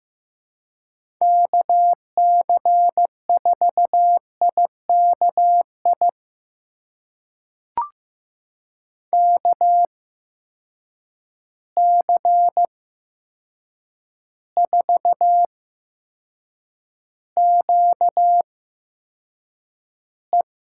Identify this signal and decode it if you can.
Morse code, heard as a keyed tone, about 15 wpm, spaced as for 7 wpm, 700 Hz. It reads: KC4IKI KC4QE
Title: USA Call Signs - 8x Character Spacing - 15wpm